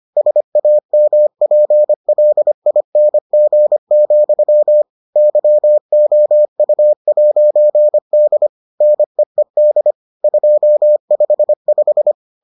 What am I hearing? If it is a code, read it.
Morse code, heard as a keyed tone, about 25 wpm, 600 Hz. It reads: SAMPLING, YOU'D NEED 255